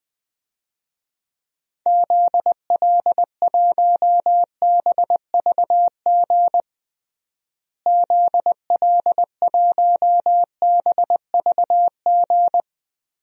Morse code, heard as a keyed tone, about 20 wpm, 700 Hz. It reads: ZL1BVG ZL1BVG
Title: DXCC-100 Call Signs - 6 Characters - Repeated 3x - 20wpm